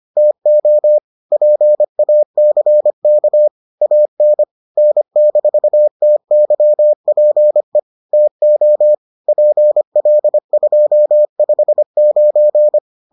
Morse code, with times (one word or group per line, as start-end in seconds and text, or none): 0.2-1.0 TO
1.3-3.5 PACK
3.8-4.4 AN
4.8-7.8 N-TYPE
8.1-8.9 TO
9.3-12.8 PL259